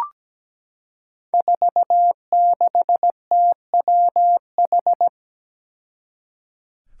Morse code, heard as a keyed tone, about 17 wpm, 700 Hz. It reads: 46TWH
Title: Five Challenging Letter-Number Combinations 17wpm